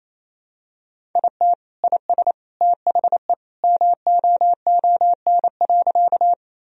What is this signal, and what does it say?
Morse code, heard as a keyed tone, about 28 wpm, 700 Hz. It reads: IT IS THE MOON.
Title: Sentences from Top 1000 Words - Review Part 3 - 28wpm